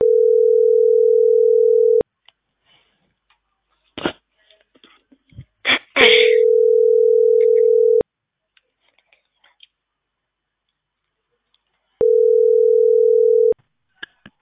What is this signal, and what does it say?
Ringback tone (North America), 440 + 480 Hz.